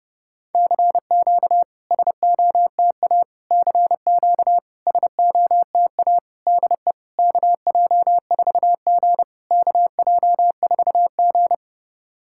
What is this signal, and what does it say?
Morse code, heard as a keyed tone, about 30 wpm, 700 Hz. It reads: CQ SOTA CQ SOTA DE KJ4G KJ4G
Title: SOTA QSO Practice 30wpm